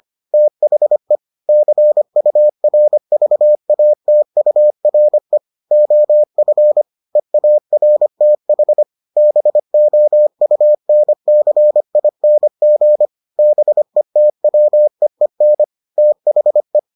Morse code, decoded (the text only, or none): THE CURVATURE OF EARTH BOUNCING BETWEEN THE